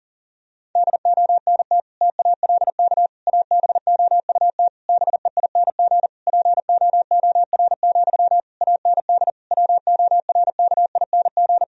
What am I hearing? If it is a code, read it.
Morse code, heard as a keyed tone, about 40 wpm, 700 Hz. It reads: DONT TALK ABOUT BEING POOR, AND WORKING
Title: Sentences from Top 1000 Words - Review Part 4 - 40wpm